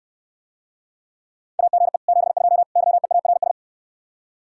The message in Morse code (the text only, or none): AGE 72 YEARS